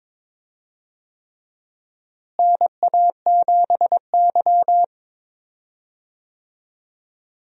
Morse code, heard as a keyed tone, about 22 wpm, 700 Hz. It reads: NA7Y
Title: USA Call Signs - 4 Characters - Repeated 3x - 22wpm